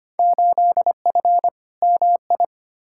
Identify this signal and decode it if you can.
Morse code, heard as a keyed tone, about 25 wpm, 700 Hz. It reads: 8F MI
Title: Field Day Exchanges Encore - 25wpm